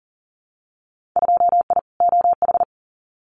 Morse code, heard as a keyed tone, about 39 wpm, 700 Hz. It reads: 2I OH
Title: Winter Field Day Exchanges 40wpm